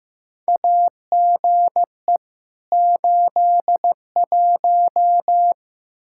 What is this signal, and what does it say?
Morse code, heard as a keyed tone, about 15 wpm, 700 Hz. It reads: AGE 81
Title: Practice QSO - Exchange 3 - More Info - 15wpm